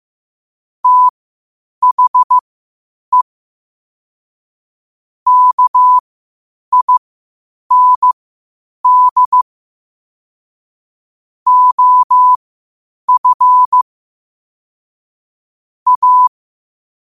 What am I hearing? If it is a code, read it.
Morse code, heard as a keyed tone, about 15 wpm, spaced as for 8 wpm, 1000 Hz. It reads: THE KIND OF A
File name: qotdfiles / 8wpm